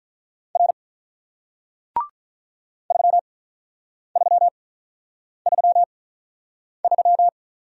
Morse code, heard as a keyed tone, about 48 wpm, spaced as for 9 wpm, 700 Hz. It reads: R 3333